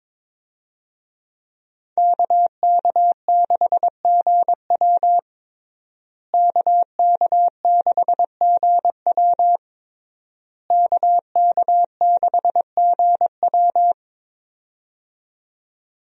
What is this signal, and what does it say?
Morse code, heard as a keyed tone, about 22 wpm, 700 Hz. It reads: KK6GW KK6GW KK6GW
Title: USA Call Signs - 5 Characters - Repeated 3x - 22wpm